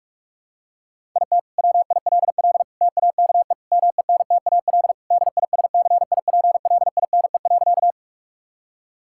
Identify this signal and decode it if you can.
Morse code, heard as a keyed tone, about 46 wpm, 700 Hz. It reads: IT WILL TAKE MENTAL DISCIPLINE.
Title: Sentences from Top 1000 Words - Review Part 6 - 45wpm